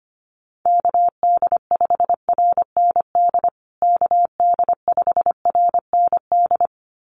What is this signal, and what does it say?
Morse code, heard as a keyed tone, about 25 wpm, 700 Hz. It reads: KD5RND KD5RND
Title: USA Call Signs Repeated 2x 25wpm